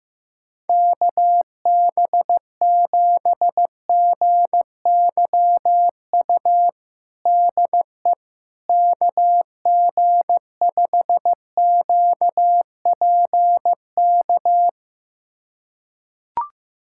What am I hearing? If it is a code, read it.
Morse code, heard as a keyed tone, about 15 wpm, 700 Hz. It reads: KB7GYU DE KG5QPK
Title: USA Callsign DE Callsign - 6 Characters - 15wpm